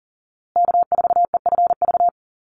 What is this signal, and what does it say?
Morse code, heard as a keyed tone, about 40 wpm, 700 Hz. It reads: K4EFV
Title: USA Call Signs Repeated 3x 40wpm